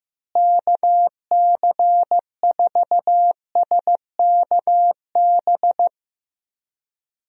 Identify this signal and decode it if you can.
Morse code, heard as a keyed tone, about 15 wpm, 700 Hz. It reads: KC4SKB
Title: USA Call Signs - 6 Characters - Repeated 2x - 15wpm